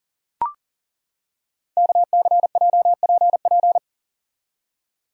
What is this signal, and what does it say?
Morse code, heard as a keyed tone, about 39 wpm, 690 Hz. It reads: KCJPP
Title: Code Groups - 5 Letters - 40wpm